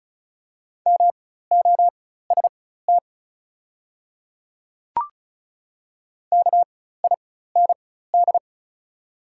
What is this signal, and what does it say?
Morse code, heard as a keyed tone, about 36 wpm, spaced as for 16 wpm, 690 Hz. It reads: MOST KIND
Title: 4x Character Spacing - Top 200 Words - 35wpm